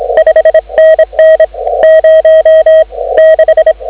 Morse code, heard as a keyed tone, about 24 wpm, 590 Hz. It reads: 5NN 0 6
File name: CW / 2017 HA-OB HA5YG